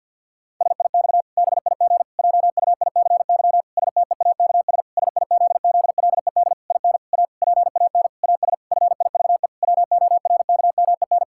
Morse code, heard as a keyed tone, about 49 wpm, 700 Hz. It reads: SIX BIG JUICY STEAKS SIZZLED IN A PAN AS FIVE WORKMEN
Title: Pangram Challenge v2 - 50wpm